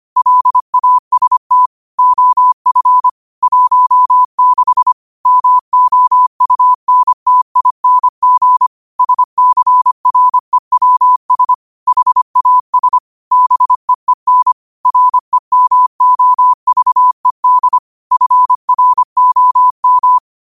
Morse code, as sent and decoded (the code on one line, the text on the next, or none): .-.. .- ... - / --- ..-. / .---- -.... / -- --- ..- -. - .. -. --. / ... -.-. .-. .-- ... / .... .- ... / -... . -. / .-. -- --- ...- . -.. / ..-. .-. --- --
LAST OF 16 MOUNTING SCREWS HAS BEEN REMOVED FROM